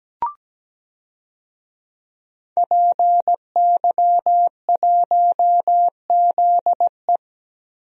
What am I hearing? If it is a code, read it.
Morse code, heard as a keyed tone, about 17 wpm, 700 Hz. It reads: PY1ZE
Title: DXCC-100 Call Signs - 5 Characters - Repeated 3x - 17wpm